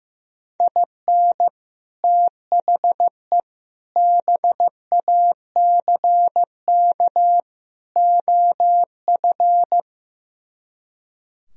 Morse code, heard as a keyed tone, about 15 wpm, 700 Hz. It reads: IN THE BACK OF
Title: Speed-Racing - Sets of 4 Words 15wpm